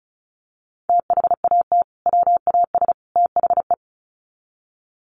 Morse code, heard as a keyed tone, about 34 wpm, 700 Hz. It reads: THAT WAS THE
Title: Sets of 3 Words 35wpm